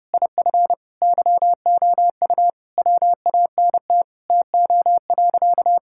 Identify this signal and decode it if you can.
Morse code, heard as a keyed tone, about 30 wpm, 700 Hz. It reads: IF YOU WANT TO.
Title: Sentences from Top 1600 Words 30wpm